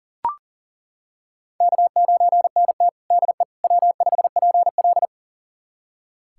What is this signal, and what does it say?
Morse code, heard as a keyed tone, about 40 wpm, 700 Hz. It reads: K9NT DE W5PL
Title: USA Callsign DE Callsign - 4 Characters - 40wpm